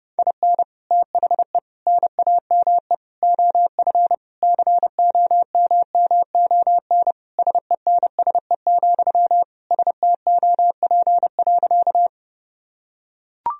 Morse code, heard as a keyed tone, about 32 wpm, 700 Hz. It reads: IN THE NAME OF COMMON SENSE, STOP.
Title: Sentences from Top 1000 Words - Review Part 4 - 30wpm